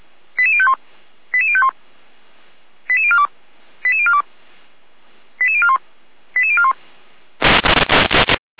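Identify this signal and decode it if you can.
Five-tone selective call (ZVEI), heard as quick runs of five tones, tones of about 70 ms. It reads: 88051 88051 88042 88042 88041 88041
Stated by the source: ZVEI five-tone selective call received on an SDR.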